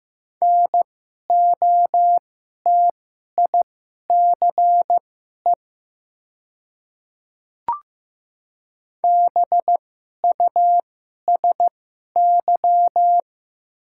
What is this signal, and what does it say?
Morse code, heard as a keyed tone, about 15 wpm, spaced as for 11 wpm, 700 Hz. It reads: NOTICE BUSY
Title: 2x Character Spacing - Top 500 Words - 15wpm